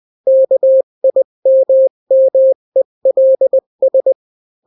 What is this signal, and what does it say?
Morse code, heard as a keyed tone, about 20 wpm, 530 Hz. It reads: KIMMELS
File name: daily / 18wpm